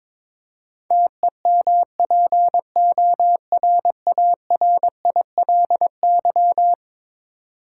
Morse code, heard as a keyed tone, about 22 wpm, 700 Hz. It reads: TEMPORARILY
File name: practice / 22wpm / Eleven Letter Words - 22wpm